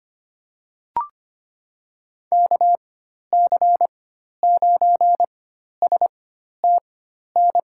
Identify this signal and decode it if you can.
Morse code, heard as a keyed tone, about 25 wpm, spaced as for 12 wpm, 700 Hz. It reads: KC9STN
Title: USA Call Signs - 6 Characters - 4x Character Spacing - 25wpm